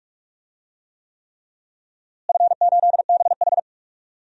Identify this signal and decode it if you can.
Morse code, heard as a keyed tone, about 46 wpm, 700 Hz. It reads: F8BH
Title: DXCC-100 Call Signs - 4 Characters - Repeated 3x - 45wpm